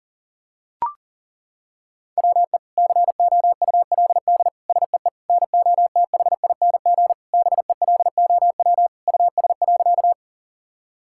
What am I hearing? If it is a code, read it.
Morse code, heard as a keyed tone, about 40 wpm, 700 Hz. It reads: WE COULD SEE NOTHING BELOW US.
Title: Sentences from Top 1000 Words - Review Part 5 - 40wpm